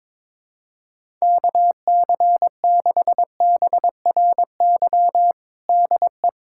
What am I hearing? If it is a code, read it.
Morse code, heard as a keyed tone, about 22 wpm, 700 Hz. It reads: KC6BRY DE
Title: USA Callsign DE Callsign - 6 Characters - 22wpm